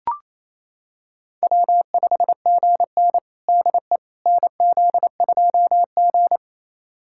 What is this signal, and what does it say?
Morse code, heard as a keyed tone, about 28 wpm, 700 Hz. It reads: W5GN DE NZ2G
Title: USA Callsign DE Callsign - 4 Characters - 28wpm